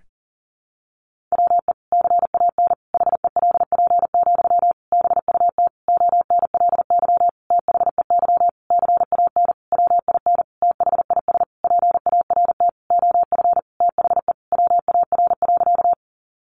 Morse code, heard as a keyed tone, about 40 wpm, 700 Hz. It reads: WE CAN HELP, BUT ONLY THEY CAN WIN THIS PART OF THE WAR.
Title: Sentences from Top 1500 Words 40wpm